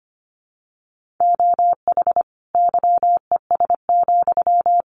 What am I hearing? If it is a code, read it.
Morse code, heard as a keyed tone, about 25 wpm, 700 Hz. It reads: OH YES,
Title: Sentences from Top 600 Words 25wpm